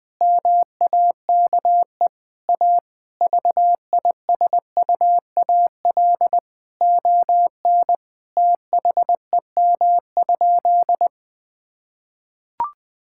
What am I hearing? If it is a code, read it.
Morse code, heard as a keyed tone, about 21 wpm, 700 Hz. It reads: MAKE A VISUAL ON THEM?
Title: Sentences from Top 1000 Words - Review Part 6 - 20wpm